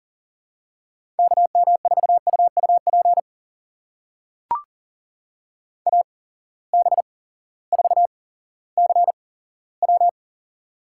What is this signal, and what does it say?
Morse code, heard as a keyed tone, about 40 wpm, spaced as for 18 wpm, 700 Hz. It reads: KM4UUP AB4CW
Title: USA Call Signs - 8x Character Spacing - 40wpm